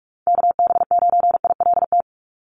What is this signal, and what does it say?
Morse code, heard as a keyed tone, about 44 wpm, 700 Hz. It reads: KB9ILT
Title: NA SOTA Chaser Callsigns 45wpm